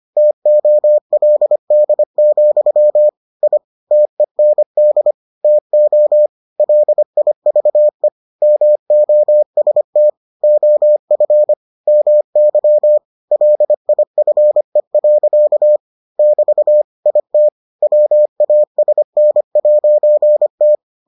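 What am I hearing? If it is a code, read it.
Morse code, heard as a keyed tone, about 25 wpm, 600 Hz. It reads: TOLD, I TEND TO LIVE MOST OF MY LIFE. <BT> IT WASN'T
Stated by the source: Morse edition of the Foundations of Amateur Radio episodes.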